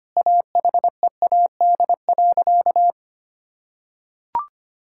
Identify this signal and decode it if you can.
Morse code, heard as a keyed tone, about 25 wpm, 700 Hz. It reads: AHEAD.
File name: practice / 25wpm / Sentences with 2-Word Phrases - 25wpm